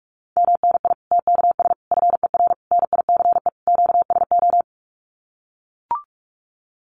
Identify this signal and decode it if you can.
Morse code, heard as a keyed tone, about 44 wpm, 700 Hz. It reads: MNI TKS FER NICE QSO
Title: Common QSO Phrases 45wpm